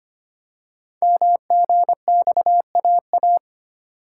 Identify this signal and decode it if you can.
Morse code, heard as a keyed tone, about 26 wpm, 700 Hz. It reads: MGXAA
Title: Code Groups - 5 Letters - 25wpm